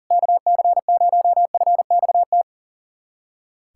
Morse code, heard as a keyed tone, about 37 wpm, 700 Hz. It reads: KC0FXT
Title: USA Callsign DE Callsign - 6 Characters - 40wpm